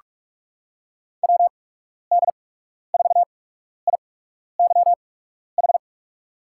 Morse code, heard as a keyed tone, about 44 wpm, spaced as for 12 wpm, 700 Hz. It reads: WD4IYH
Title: USA Call Signs - 6 Characters - 8x Character Spacing - 45wpm